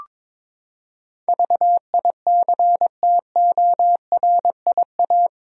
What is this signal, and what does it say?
Morse code, heard as a keyed tone, about 22 wpm, 700 Hz. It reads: VICTORIA
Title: USA Common Names - 22wpm